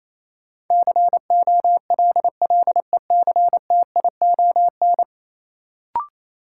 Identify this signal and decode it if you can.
Morse code, heard as a keyed tone, about 28 wpm, 700 Hz. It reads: COLLECTION
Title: Sentences from Top 1200 Words - New Word Drill - 28wpm